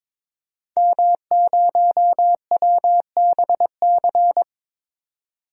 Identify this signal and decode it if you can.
Morse code, heard as a keyed tone, about 22 wpm, 700 Hz. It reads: M0WBC
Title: DXCC-100 Call Signs - 5 Characters - 22wpm